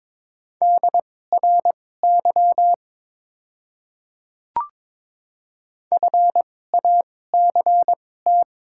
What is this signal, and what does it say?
Morse code, heard as a keyed tone, about 22 wpm, spaced as for 16 wpm, 690 Hz. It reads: DRY FACT
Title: 2x Character Spacing - Top 500 Words - 22wpm